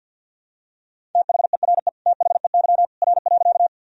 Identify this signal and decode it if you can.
Morse code, heard as a keyed tone, about 50 wpm, 700 Hz. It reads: THERE THEY R.
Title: Sentences from Top 100 Words and Abbreviations 50wpm